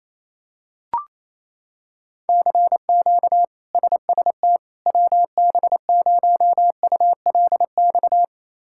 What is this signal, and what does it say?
Morse code, heard as a keyed tone, about 27 wpm, 700 Hz. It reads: CQ SST WB0ULX
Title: K1USN SST QSO Practice - 28wpm